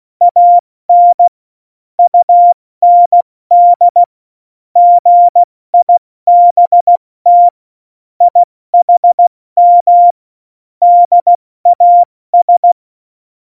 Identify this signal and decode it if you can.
Morse code, heard as a keyed tone, about 16 wpm, 700 Hz. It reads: AN UND GIBT IHM DAS
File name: Morsekurs / Witze / Witz 04. September 2018, 70 F80